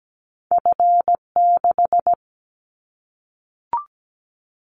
Morse code, heard as a keyed tone, about 17 wpm, 700 Hz. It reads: F6
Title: Callsign Prefixes for Top 25 DXCC - Repeated 3x - 17wpm